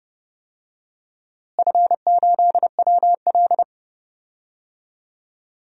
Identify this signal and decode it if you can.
Morse code, heard as a keyed tone, about 30 wpm, 700 Hz. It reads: F8WL
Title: DXCC-100 Call Signs - 4 Characters - Repeated 2x - 30wpm